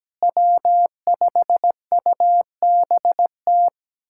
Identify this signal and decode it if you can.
Morse code, heard as a keyed tone, about 17 wpm, 700 Hz. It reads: W5UBT
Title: USA Callsign DE Callsign - 5 Characters - 17wpm